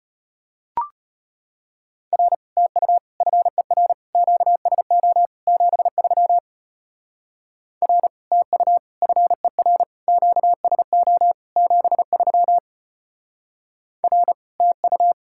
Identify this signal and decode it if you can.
Morse code, heard as a keyed tone, about 33 wpm, 700 Hz. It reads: R TU FER QSO 73 R TU FER QSO 73 R TU